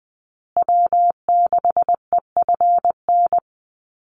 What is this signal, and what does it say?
Morse code, heard as a keyed tone, about 20 wpm, 700 Hz. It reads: W6EFN
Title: USA Call Signs Repeated 3x 20wpm